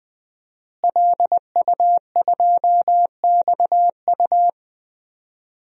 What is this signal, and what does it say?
Morse code, heard as a keyed tone, about 20 wpm, 700 Hz. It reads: LU2XU
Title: DXCC-100 Call Signs - 5 Characters - Repeated 2x - 20wpm